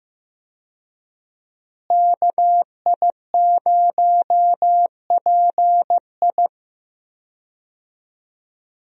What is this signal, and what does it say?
Morse code, heard as a keyed tone, about 15 wpm, 700 Hz. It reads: KI0PI
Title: USA Call Signs - 5 Characters - Repeated 2x - 15wpm